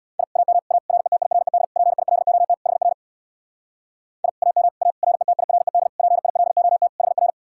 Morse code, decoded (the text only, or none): I AM A RETIRED PILOT HR I AM A RETIRED PILOT HR